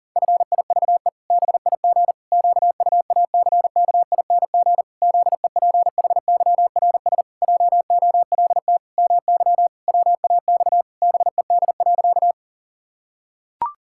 Morse code, heard as a keyed tone, about 40 wpm, 700 Hz. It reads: FIVE BIG QUACKING ZEPHYRS JOLT MY WAX BED.
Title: Pangram Challenge v2 - 40wpm